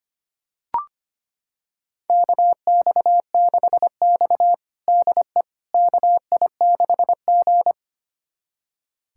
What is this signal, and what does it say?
Morse code, heard as a keyed tone, about 25 wpm, 700 Hz. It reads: KX6X DE KI6G
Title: USA Callsign DE Callsign - 4 Characters - 25wpm